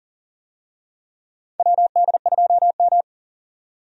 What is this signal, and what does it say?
Morse code, heard as a keyed tone, about 40 wpm, 700 Hz. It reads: WD2M
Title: USA Call Signs - 4 Characters - 40wpm